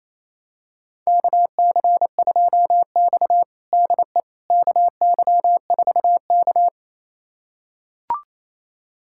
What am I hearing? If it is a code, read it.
Morse code, heard as a keyed tone, about 28 wpm, 700 Hz. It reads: KC2X DE KY4K
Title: USA Callsign DE Callsign - 4 Characters - 28wpm